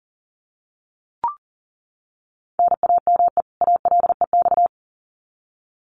Morse code, heard as a keyed tone, about 40 wpm, 700 Hz. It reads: NAME ALEX
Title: Common QSO Phrases 40wpm